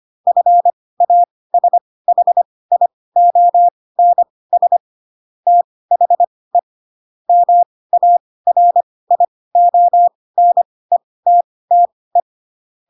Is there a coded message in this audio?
Yes: Morse code from 0.3 to 12.2 s, heard as a keyed tone, about 25 wpm, spaced as for 18 wpm, 700 Hz. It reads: FASHIONS THE MARIONETTE